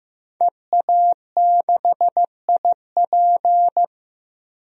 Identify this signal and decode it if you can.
Morse code, heard as a keyed tone, about 15 wpm, 700 Hz. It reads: EA6IP
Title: DXCC-100 Call Signs - 5 Characters - Repeated 3x - 15wpm